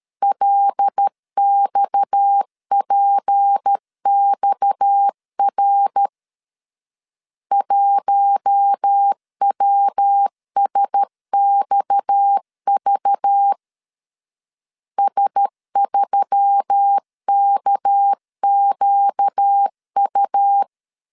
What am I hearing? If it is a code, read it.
Morse code, heard as a keyed tone, about 13 wpm, 800 Hz. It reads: LXPXR 1WSXV S3KQU
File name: LEZIONI / 6 PKX0 / 6 PKX 12wpm 2024